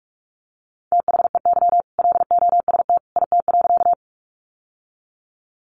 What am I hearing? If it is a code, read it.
Morse code, heard as a keyed tone, about 44 wpm, 700 Hz. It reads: THEY LOST IT.
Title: Sentences from Top 1400 Words 45wpm